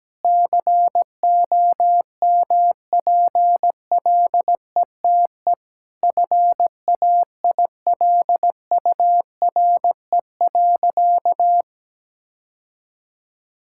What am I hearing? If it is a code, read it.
Morse code, heard as a keyed tone, about 17 wpm, 700 Hz. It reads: COMPLETE FAILURE.
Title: Sentences from Top 1000 Words - Review Part 6 - 17wpm